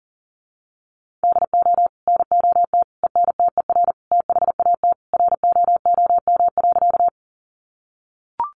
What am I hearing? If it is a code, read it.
Morse code, heard as a keyed tone, about 40 wpm, 700 Hz. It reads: DO NOT ENTER THAT ROOM.
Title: Sentences from Top 1300 Words 40wpm